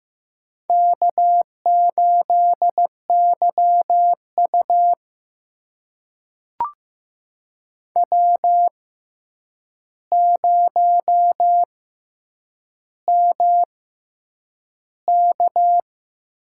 Morse code, heard as a keyed tone, about 15 wpm, spaced as for 8 wpm, 700 Hz. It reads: K8YU W0MK
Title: USA Call Signs - 4 Characters - 6x Character Spacing - 15wpm